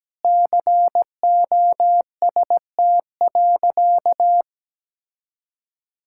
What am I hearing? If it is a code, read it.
Morse code, heard as a keyed tone, about 17 wpm, 700 Hz. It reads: COST.
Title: Sentences from Top 1000 Words - Review Part 6 - 17wpm